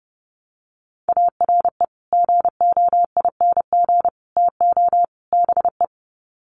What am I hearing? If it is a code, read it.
Morse code, heard as a keyed tone, about 30 wpm, 700 Hz. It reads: ARE GOING TO BE
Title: Sets of 4 Words 30wpm